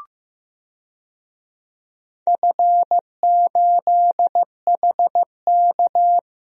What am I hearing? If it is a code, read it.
Morse code, heard as a keyed tone, about 15 wpm, 700 Hz. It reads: F8HK
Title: DXCC-100 Call Signs - 4 Characters - 15wpm